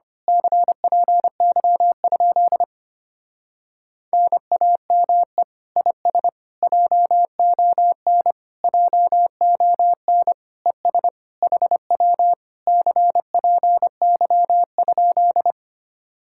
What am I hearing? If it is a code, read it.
Morse code, heard as a keyed tone, about 24 wpm, 700 Hz. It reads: CPY? NAME IS JON JON ES HW CPY?
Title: Speed-Racing Common QSO Phrases - 25wpm